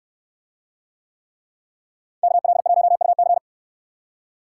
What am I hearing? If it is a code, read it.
Morse code, heard as a keyed tone, about 69 wpm, 700 Hz. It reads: KL1UC